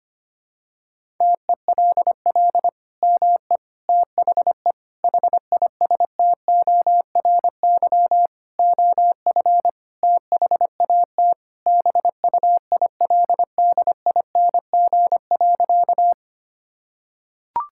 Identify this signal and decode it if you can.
Morse code, heard as a keyed tone, about 25 wpm, 700 Hz. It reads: TELL ME THE HISTORY OF THAT BUILDING.
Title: Sentences from Top 1000 Words - Review Part 5 - 25wpm